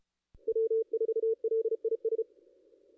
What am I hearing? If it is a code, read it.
Morse code, heard as a keyed tone, about 34 wpm, 450 Hz. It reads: W4LIS
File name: contests / ss-cw-k5tr-op-n5rz -2200